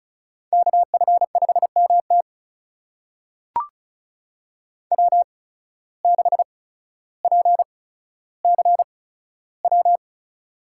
Morse code, heard as a keyed tone, about 36 wpm, spaced as for 15 wpm, 700 Hz. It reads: KF5MT W6PCW
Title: USA Call Signs - 5 Characters - 8x Character Spacing - 35wpm